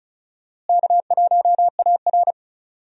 Morse code, heard as a keyed tone, about 34 wpm, 700 Hz. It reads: K1AR
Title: USA Call Signs - 4 Characters - Repeated 3x - 35wpm